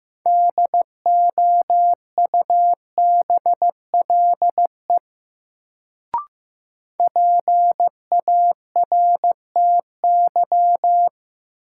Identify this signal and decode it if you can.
Morse code, heard as a keyed tone, about 16 wpm, 700 Hz. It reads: DOUBLE PARTY
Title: Sentences from Top 1400 Words - New Word Drill - 15wpm